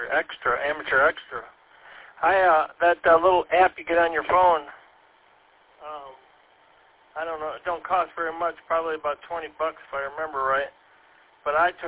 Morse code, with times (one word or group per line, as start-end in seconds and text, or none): none